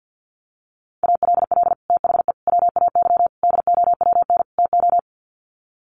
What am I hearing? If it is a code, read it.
Morse code, heard as a keyed tone, about 48 wpm, 700 Hz. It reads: ALL THE WAY DOWN TO
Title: Sets of 5 Words Encore2 50wpm